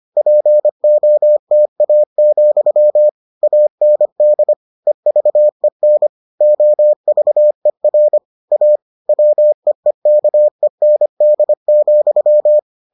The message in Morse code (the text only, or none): POTA, AND EVEN OVER A WEEKEND,